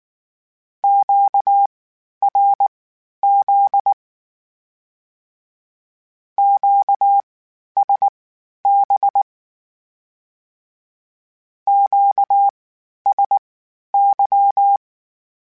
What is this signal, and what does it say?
Morse code, heard as a keyed tone, about 19 wpm, spaced as for 11 wpm, 800 Hz. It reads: QRZ QSB QSY
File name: (spaced) / Q-Calls Spaced-mp3s / Q Calls sp Once @18